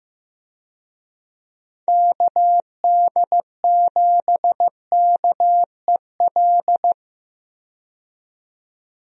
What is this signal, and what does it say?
Morse code, heard as a keyed tone, about 15 wpm, 700 Hz. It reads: KD7KEL
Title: USA Call Signs - 6 Characters - 15wpm